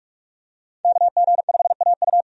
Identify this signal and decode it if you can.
Morse code, heard as a keyed tone, about 44 wpm, 700 Hz. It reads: KG5AU
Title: USA Call Signs - 5 Characters - 45wpm